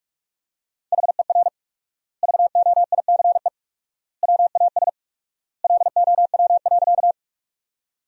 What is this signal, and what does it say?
Morse code, heard as a keyed tone, about 42 wpm, 700 Hz. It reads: HER VOICE WAS LOW.
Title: 4x Word-Spacing - Sentences from Top 500 Words - 45wpm